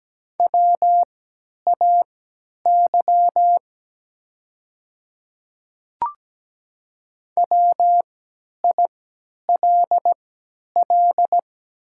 Morse code, heard as a keyed tone, about 17 wpm, spaced as for 10 wpm, 700 Hz. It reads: WAY WILL
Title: 3x Character Spacing - Top 100 Words - 17wpm